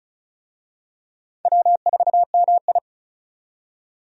Morse code, heard as a keyed tone, about 36 wpm, 700 Hz. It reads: W4MI
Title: USA Call Signs - 4 Characters - 35wpm